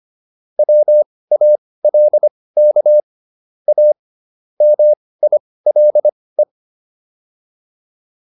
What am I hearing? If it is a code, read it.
Morse code, heard as a keyed tone, about 25 wpm, spaced as for 18 wpm, 600 Hz. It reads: WALK A MILE